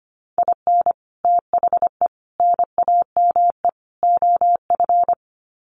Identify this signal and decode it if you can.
Morse code, heard as a keyed tone, about 25 wpm, 700 Hz. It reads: IN THE NAME OF
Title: Sets of 4 Words 25wpm